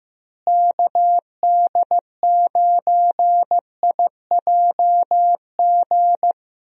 Morse code, heard as a keyed tone, about 15 wpm, 700 Hz. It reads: KD9IJG
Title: USA Call Signs - 6 Characters - Repeated 3x - 15wpm